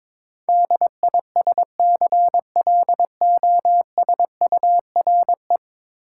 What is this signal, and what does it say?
Morse code, heard as a keyed tone, about 21 wpm, 700 Hz. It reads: DISCLOSURE
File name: practice / 22wpm / 10 Letter Words - First 3 Letters Given - 22wpm